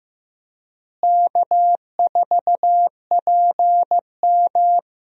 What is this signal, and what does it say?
Morse code, heard as a keyed tone, about 15 wpm, 700 Hz. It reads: K4PM
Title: USA Call Signs - 4 Characters - Repeated 3x - 15wpm